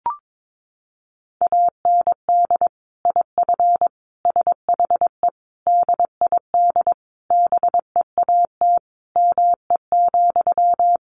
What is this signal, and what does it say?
Morse code, heard as a keyed tone, about 22 wpm, 700 Hz. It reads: AND IF SHE DID BEAT ME,
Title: Sentences from Top 1300 Words 22wpm